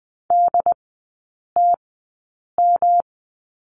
Morse code, heard as a keyed tone, about 20 wpm, spaced as for 8 wpm, 700 Hz. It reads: DTM
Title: 107 Lesson 18e - Letter m - Three Characters 20wpm